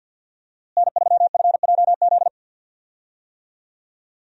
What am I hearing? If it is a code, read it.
Morse code, heard as a keyed tone, about 50 wpm, 700 Hz. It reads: N3FJZ
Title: USA Call Signs - 5 Characters - Repeated 2x - 50wpm